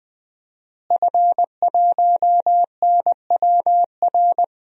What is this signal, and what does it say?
Morse code, heard as a keyed tone, about 21 wpm, 700 Hz. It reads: F1NWR
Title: DXCC-100 Call Signs - 5 Characters - Repeated 2x - 20wpm